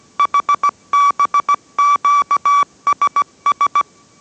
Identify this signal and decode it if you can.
Morse code, heard as a keyed tone, about 17 wpm, 1150 Hz. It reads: HBQSS